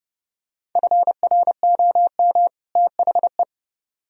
Morse code, heard as a keyed tone, about 29 wpm, 700 Hz. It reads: FROM THE